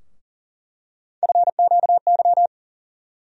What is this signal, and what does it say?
Morse code, heard as a keyed tone, about 40 wpm, 700 Hz. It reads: FQY
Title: USA Call Sign Postfixes - 3 Letters 40wpm